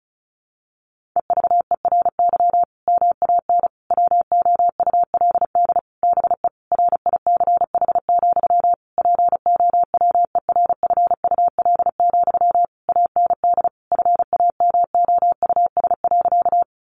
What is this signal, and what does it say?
Morse code, heard as a keyed tone, about 34 wpm, 700 Hz. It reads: EVERY MAN WOULD BE RICH, POWERFUL, AND FAMOUS.
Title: Sentences from Top 1500 Words 35wpm